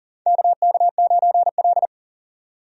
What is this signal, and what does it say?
Morse code, heard as a keyed tone, about 40 wpm, 700 Hz. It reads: KK9L
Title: USA Call Signs - 4 Characters - 40wpm